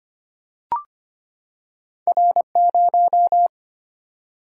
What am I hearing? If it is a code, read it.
Morse code, heard as a keyed tone, about 25 wpm, 700 Hz. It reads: R0
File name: Two Letter-Number Combinations - 25wpm